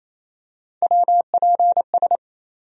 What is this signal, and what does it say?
Morse code, heard as a keyed tone, about 27 wpm, 700 Hz. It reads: WPS
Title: USA Call Sign Postfixes - 3 Letters 28wpm